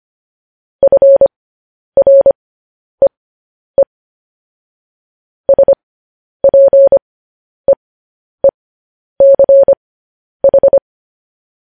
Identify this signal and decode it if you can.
Morse code, heard as a keyed tone, about 25 wpm, spaced as for 10 wpm, 550 Hz. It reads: FREE SPEECH
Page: news-headlines-in-morse-code-at-10-wpm